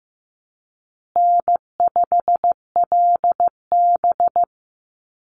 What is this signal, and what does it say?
Morse code, heard as a keyed tone, about 15 wpm, 700 Hz. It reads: N5LB
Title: Easy Call Signs Repeated 3x - 15wpm